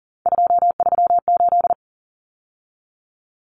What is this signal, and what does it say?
Morse code, heard as a keyed tone, about 40 wpm, 700 Hz. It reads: 238
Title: Numbers - 3-Digit 40wpm